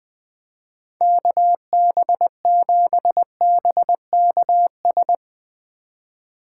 Morse code, heard as a keyed tone, about 20 wpm, 700 Hz. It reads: KB7BKS
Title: USA Call Signs - 2x Character Spacing - 20wpm